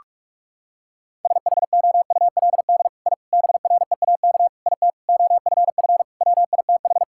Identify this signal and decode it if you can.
Morse code, heard as a keyed tone, about 46 wpm, 700 Hz. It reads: SHOULD I BREAK IT OFF WITH
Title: Sentences from Top 1000 Words - Review Part 6 - 45wpm